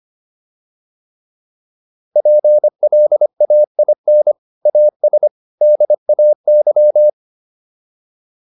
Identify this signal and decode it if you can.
Morse code, heard as a keyed tone, about 25 wpm, 600 Hz. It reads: PLAIN AS DAY
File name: PR403 25